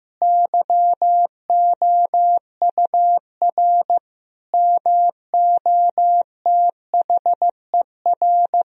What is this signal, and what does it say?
Morse code, heard as a keyed tone, about 16 wpm, 700 Hz. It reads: YOUR MOTHER
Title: Sentences from Top 1000 Words - Review Part 3 - 15wpm